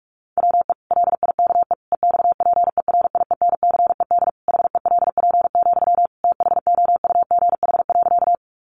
Morse code, heard as a keyed tone, about 44 wpm, 700 Hz. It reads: WE LIKE EXPERIENCED HELP, THOUGH.
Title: Sentences from Top 1100 Words 45wpm